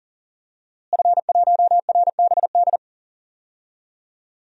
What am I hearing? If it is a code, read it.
Morse code, heard as a keyed tone, about 37 wpm, 700 Hz. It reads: F1RBD
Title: DXCC-100 Call Signs - 5 Characters - Repeated 2x - 40wpm